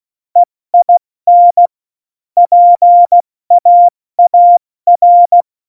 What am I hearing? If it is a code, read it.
Morse code, heard as a keyed tone, about 16 wpm, 700 Hz. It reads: EIN PAAR
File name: Morsekurs / Witze / Witz 24. Oktober 17, Tempo 70